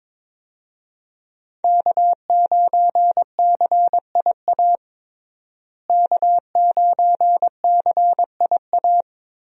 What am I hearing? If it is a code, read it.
Morse code, heard as a keyed tone, about 22 wpm, 700 Hz. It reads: K9CIA K9CIA
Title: USA Call Signs - 5 Characters - Repeated 3x - 22wpm